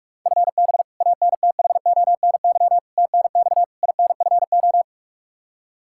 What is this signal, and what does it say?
Morse code, heard as a keyed tone, about 41 wpm, 700 Hz. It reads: FB ANTHONY TNX INFO
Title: Practice QSO - Exchange 3 - More Info - 45wpm